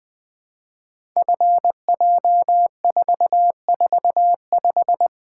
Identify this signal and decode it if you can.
Morse code, heard as a keyed tone, about 20 wpm, 700 Hz. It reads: FJ445
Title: Code Groups - 5 Letter:Numbers - 20wpm